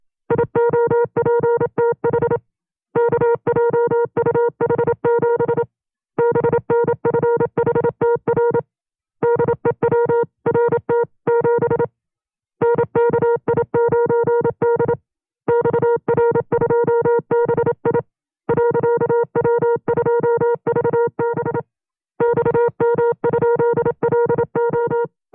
Morse code, heard as a keyed tone, about 27 wpm, 470 Hz. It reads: IOPTH KJUH7 BNFHTR DEWRT7 NKI9D XR2BI .W2VB XM?LO
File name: 25WPM